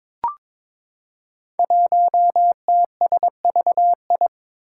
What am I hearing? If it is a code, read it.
Morse code, heard as a keyed tone, about 22 wpm, 700 Hz. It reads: 1TSVI
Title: Five Challenging Letter-Number Combinations 22wpm